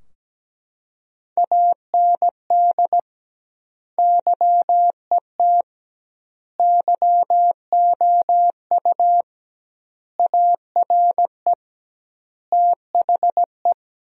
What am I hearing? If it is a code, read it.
Morse code, heard as a keyed tone, about 17 wpm, 700 Hz. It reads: AND YET YOU ARE THE
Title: 2x Word-Spacing - Sentences from Top 500 Words - 17wpm